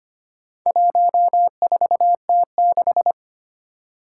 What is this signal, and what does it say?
Morse code, heard as a keyed tone, about 25 wpm, 700 Hz. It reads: 14T6
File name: practice / 25wpm / Four Letter-Number Combinations - 25wpm